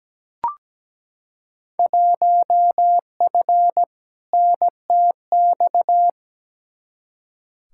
Morse code, heard as a keyed tone, about 17 wpm, 700 Hz. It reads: 1F NTX
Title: Field Day Exchanges Encore - 17wpm